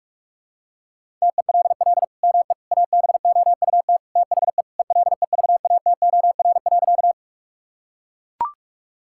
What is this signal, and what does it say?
Morse code, heard as a keyed tone, about 43 wpm, 700 Hz. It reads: TELL ME ABOUT THE ELEVATOR.
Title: Sentences from Top 1600 Words 45wpm